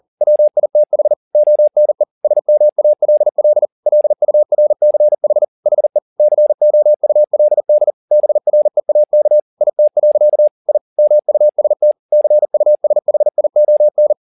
Morse code, heard as a keyed tone, about 40 wpm, 600 Hz. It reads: WITH ONE SMALL LURCH HE COULD BREAK IT. I MUST CUSHION
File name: the old man and the sea 2 40wpm